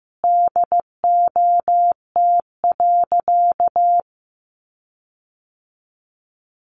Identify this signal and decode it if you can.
Morse code, heard as a keyed tone, about 15 wpm, 700 Hz. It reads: DOT.
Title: Sentences from Top 1500 Words 15wpm